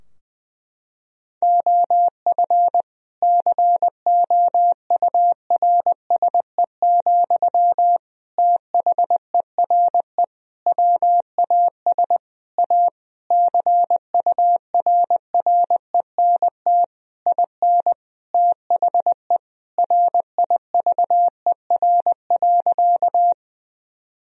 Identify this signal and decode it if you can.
Morse code, heard as a keyed tone, about 20 wpm, 700 Hz. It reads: OF COURSE, THERE WAS A CURRENT IN THE RIVER.
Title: Sentences from Top 1000 Words - Review Part 4 - 20wpm